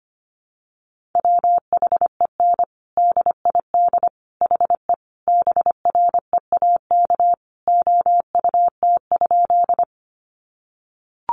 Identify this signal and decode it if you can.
Morse code, heard as a keyed tone, about 25 wpm, 700 Hz. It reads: WHEN DID HE BREAK OUT?
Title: Sentences from Top 900 Words 25wpm